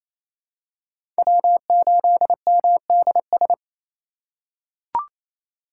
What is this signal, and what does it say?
Morse code, heard as a keyed tone, about 28 wpm, 700 Hz. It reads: W8MDS
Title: USA Call Signs - 3x Character Spacing - 28wpm